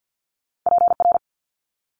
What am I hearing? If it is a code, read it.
Morse code, heard as a keyed tone, about 50 wpm, 700 Hz. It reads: PR
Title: Multi-letter Phonemes Quick Fire 50wpm